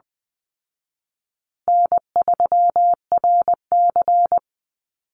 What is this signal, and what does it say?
Morse code, heard as a keyed tone, about 20 wpm, 700 Hz. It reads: N3RC
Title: Call Signs Repeated 2x - 20wpm